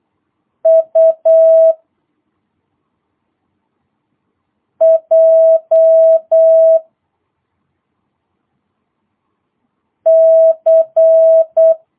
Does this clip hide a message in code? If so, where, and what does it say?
Yes: Morse code from 0.7 to 11.7 s, heard as a keyed tone, about 8 wpm, 650 Hz. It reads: U J C